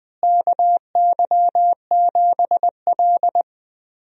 Morse code, heard as a keyed tone, about 20 wpm, 700 Hz. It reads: KY7L
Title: USA Call Signs - 4 Characters - Repeated 2x - 20wpm